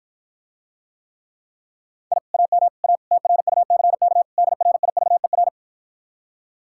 Morse code, heard as a keyed tone, about 53 wpm, 700 Hz. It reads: I AM A TRUCK DRIVER